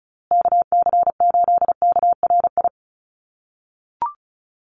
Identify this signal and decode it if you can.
Morse code, heard as a keyed tone, about 34 wpm, 700 Hz. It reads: KC8KRI
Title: USA Call Signs Repeated 3x 35wpm